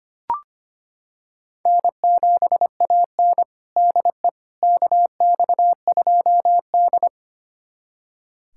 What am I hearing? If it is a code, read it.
Morse code, heard as a keyed tone, about 25 wpm, 700 Hz. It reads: N7AN DE KX2D
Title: USA Callsign DE Callsign - 4 Characters - 25wpm